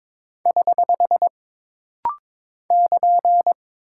Morse code, heard as a keyed tone, about 22 wpm, 700 Hz. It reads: <HH> <KN>
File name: practice / 22wpm / Single Punctuation-Prosign - Rapid-Fire - 22wpm